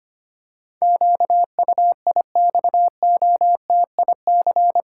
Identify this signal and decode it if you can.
Morse code, heard as a keyed tone, about 25 wpm, 700 Hz. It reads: QUIXOTIC
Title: Pangram Challenge v2 - 25wpm